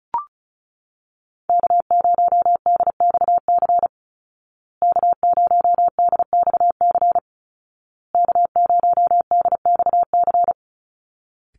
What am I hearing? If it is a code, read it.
Morse code, heard as a keyed tone, about 34 wpm, 700 Hz. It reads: K0DXC K0DXC K0DXC
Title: Call Signs Repeated 2x - 35wpm